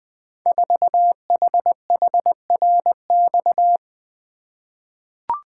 Morse code, heard as a keyed tone, about 20 wpm, 700 Hz. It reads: 4HHRX
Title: Code Groups - 5 Letter:Numbers - 20wpm